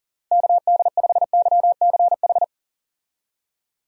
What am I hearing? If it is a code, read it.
Morse code, heard as a keyed tone, about 37 wpm, 700 Hz. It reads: KD5YCH
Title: USA Call Signs - 6 Characters - Repeated 3x - 40wpm